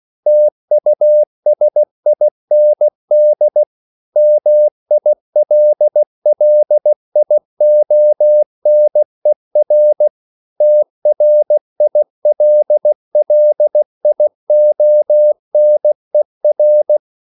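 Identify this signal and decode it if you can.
Morse code, heard as a keyed tone, about 16 wpm, 600 Hz. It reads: TUSIND MILLIONER TRILLIONER
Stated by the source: Slow (QRS) Morse code at 16 wpm, from CW med Gnister.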